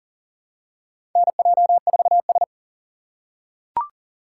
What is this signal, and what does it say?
Morse code, heard as a keyed tone, about 40 wpm, 700 Hz. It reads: NJ4S
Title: USA Call Signs - 3x Character Spacing - 40wpm